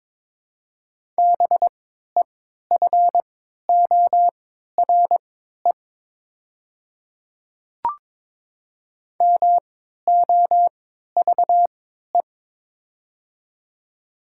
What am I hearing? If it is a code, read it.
Morse code, heard as a keyed tone, about 22 wpm, spaced as for 12 wpm, 700 Hz. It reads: BEFORE MOVE
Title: 3x Character Spacing - Top 200 Words - 22wpm